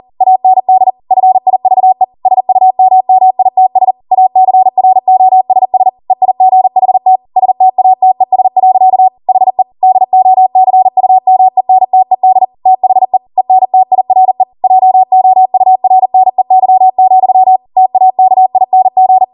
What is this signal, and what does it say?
Morse code, heard as a keyed tone, about 40 wpm, 750 Hz. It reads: AND FIVE SUMMITS ACROSS EIGHT STATES. HE DOCUMENTED THE ENTIRE JOURNEY, TAKING